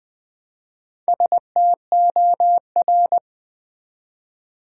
Morse code, heard as a keyed tone, about 20 wpm, 700 Hz. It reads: STOR
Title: Build A Word - 1 Letter At A Time - 5-Letter Words - 20wpm